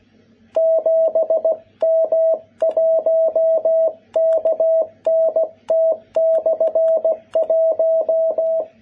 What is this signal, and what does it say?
Morse code, heard as a keyed tone, about 16 wpm, 660 Hz. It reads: 7M1KNT/1
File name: jh8cbh / 2021cw